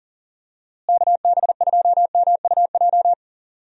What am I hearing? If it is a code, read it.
Morse code, heard as a keyed tone, about 40 wpm, 700 Hz. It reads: KB2MUJ
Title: USA Call Signs - 6 Characters - Repeated 3x - 40wpm